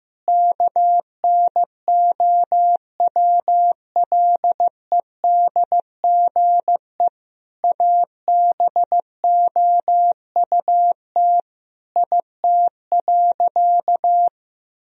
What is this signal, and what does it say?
Morse code, heard as a keyed tone, about 16 wpm, 700 Hz. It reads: KNOWLEDGE ABOUT IT.
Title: Sentences from Top 1000 Words - Review Part 3 - 15wpm